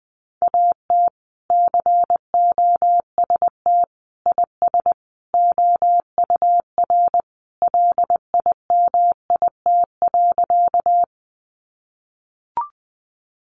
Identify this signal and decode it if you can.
Morse code, heard as a keyed tone, about 20 wpm, 700 Hz. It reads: AT COST IS OUR LIMIT.
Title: Sentences from Top 1500 Words 20wpm